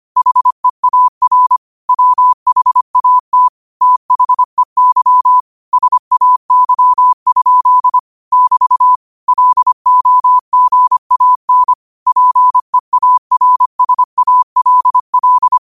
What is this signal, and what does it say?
Morse code, heard as a keyed tone, about 25 wpm, 1000 Hz. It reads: HEAR WHAT THEY SAY? <BT> LOGAN PEARSALL